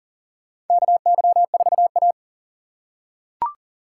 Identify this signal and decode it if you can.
Morse code, heard as a keyed tone, about 40 wpm, 700 Hz. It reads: KY4A
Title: USA Call Signs - 4 Characters - 8x Character Spacing - 40wpm